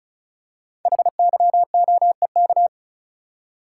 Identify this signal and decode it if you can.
Morse code, heard as a keyed tone, about 36 wpm, 700 Hz. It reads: HYOEK
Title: Code Groups - 5 Letters - 35wpm